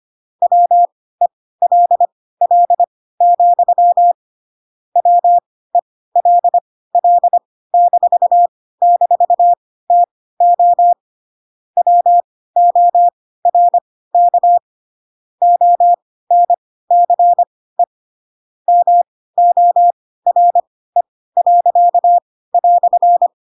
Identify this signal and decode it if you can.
Morse code, heard as a keyed tone, about 25 wpm, spaced as for 16 wpm, 700 Hz. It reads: WELL, WELL--TO WORK ONCE MORE."
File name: Pinocchio / Pinocchio ch1 10 25 ik1hge